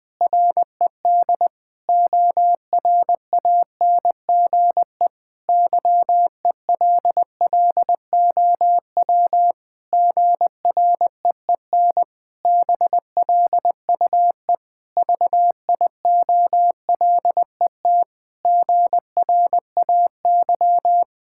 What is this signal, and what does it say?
Morse code, heard as a keyed tone, about 20 wpm, 700 Hz. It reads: RED ORANGE YELLOW GREEN BLUE VIOLET GRAY
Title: Pangram Challenge v2 - 20wpm